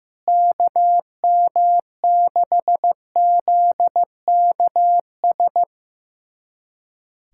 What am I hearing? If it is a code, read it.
Morse code, heard as a keyed tone, about 15 wpm, 700 Hz. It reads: KM6ZKS
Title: USA Callsign DE Callsign 15wpm